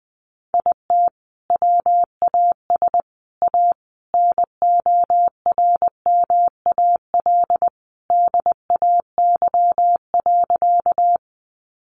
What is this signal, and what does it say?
Morse code, heard as a keyed tone, about 20 wpm, 700 Hz. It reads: IT WAS A NORMAL DAY.
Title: Sentences from Top 1200 Words 20wpm